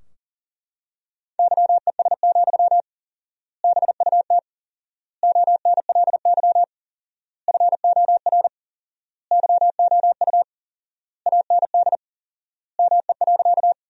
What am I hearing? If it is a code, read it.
Morse code, heard as a keyed tone, about 37 wpm, 700 Hz. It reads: YES, BUT ONLY FOR YOU AND ME.
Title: 4x Word-Spacing - Sentences from Top 500 Words - 40wpm